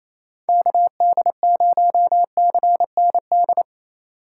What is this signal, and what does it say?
Morse code, heard as a keyed tone, about 28 wpm, 700 Hz. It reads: KD0CND
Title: USA Call Signs - 4x Character Spacing - 28wpm